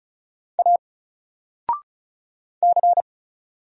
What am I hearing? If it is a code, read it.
Morse code, heard as a keyed tone, about 36 wpm, 700 Hz. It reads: A C
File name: Single-Letter-Number-35wpm